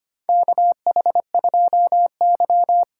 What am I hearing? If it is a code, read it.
Morse code, heard as a keyed tone, about 25 wpm, 700 Hz. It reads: KH2Y
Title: USA Call Signs - 4 Characters - 4x Character Spacing - 25wpm